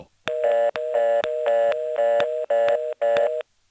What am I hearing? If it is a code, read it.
Reorder tone (North America), 480 + 620 Hz.
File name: phonetrips / SilerCitySxSNCHQ